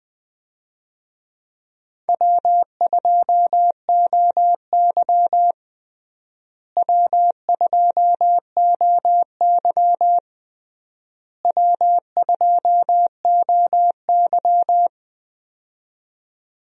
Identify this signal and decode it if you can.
Morse code, heard as a keyed tone, about 20 wpm, 700 Hz. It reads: W2OY W2OY W2OY
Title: USA Call Signs - 4 Characters - Repeated 3x - 20wpm